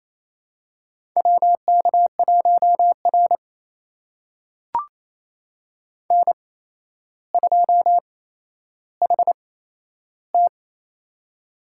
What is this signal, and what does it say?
Morse code, heard as a keyed tone, about 28 wpm, spaced as for 12 wpm, 700 Hz. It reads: WK1R N2HT
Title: USA Call Signs - 4 Characters - 8x Character Spacing - 28wpm